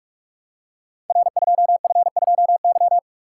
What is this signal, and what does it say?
Morse code, heard as a keyed tone, about 44 wpm, 700 Hz. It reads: R2F2Y
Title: Code Groups - 5 Letter:Numbers - 45wpm